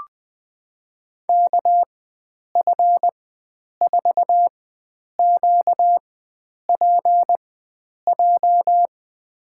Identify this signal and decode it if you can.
Morse code, heard as a keyed tone, about 20 wpm, spaced as for 9 wpm, 700 Hz. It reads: KF4QPJ
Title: USA Call Signs - 4x Character Spacing - 20wpm